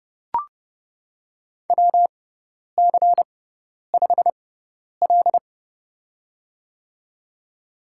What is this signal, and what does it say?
Morse code, heard as a keyed tone, about 29 wpm, spaced as for 10 wpm, 700 Hz. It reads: WC5L
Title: USA Call Signs - 4 Characters - 6x Character Spacing - 30wpm